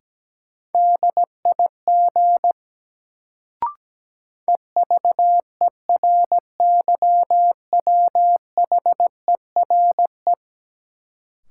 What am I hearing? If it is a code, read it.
Morse code, heard as a keyed tone, about 17 wpm, 700 Hz. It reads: DIG EVERYWHERE
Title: Sentences from Top 1600 Words - New Word Drill - 17wpm